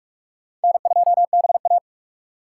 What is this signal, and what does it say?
Morse code, heard as a keyed tone, about 46 wpm, 700 Hz. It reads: N2BA
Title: USA Call Signs - 4 Characters - 2x Character Spacing - 45wpm